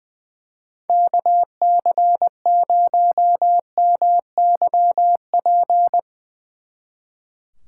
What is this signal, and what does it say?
Morse code, heard as a keyed tone, about 23 wpm, 700 Hz. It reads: KC0MYP
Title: Speed Racing - USA 2x3 Call Signs 20wpm